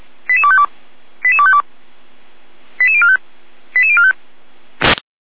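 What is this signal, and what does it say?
Five-tone selective call (ZVEI), heard as quick runs of five tones, tones of about 80 ms. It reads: 88252 88252 88045 88045
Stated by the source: ZVEI five-tone selective call received by SDR.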